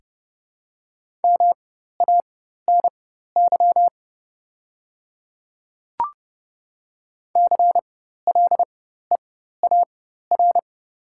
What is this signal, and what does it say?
Morse code, heard as a keyed tone, about 30 wpm, spaced as for 14 wpm, 690 Hz. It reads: MANY CLEAR
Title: 4x Character Spacing - Top 500 Words - 30wpm